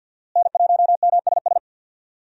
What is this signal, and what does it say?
Morse code, heard as a keyed tone, about 52 wpm, 700 Hz. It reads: N1MSS
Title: USA Call Signs - 5 Characters - Repeated 2x - 50wpm